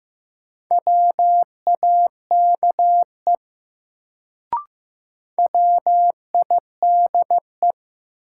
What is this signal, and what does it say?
Morse code, heard as a keyed tone, about 16 wpm, 700 Hz. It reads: WAKE WIDE
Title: Sets of 4 Words - New Word Drill - 15wpm